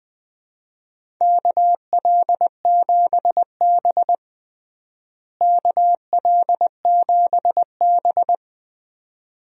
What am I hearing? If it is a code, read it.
Morse code, heard as a keyed tone, about 20 wpm, 700 Hz. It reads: KL7B KL7B
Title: DXCC-100 Call Signs - 4 Characters - Repeated 2x - 20wpm